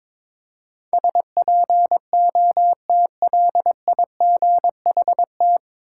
Morse code, heard as a keyed tone, about 22 wpm, 700 Hz. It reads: SPOTLIGHT
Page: Morse Code Ninja practice 22wpm